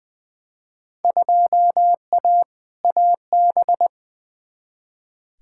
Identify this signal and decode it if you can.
Morse code, heard as a keyed tone, about 20 wpm, 700 Hz. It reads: 2A AB
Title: Field Day Exchanges - 20wpm